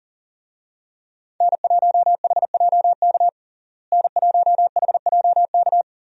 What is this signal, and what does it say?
Morse code, heard as a keyed tone, about 37 wpm, 700 Hz. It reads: N1HJK N1HJK
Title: USA Call Signs - 5 Characters - Repeated 2x - 40wpm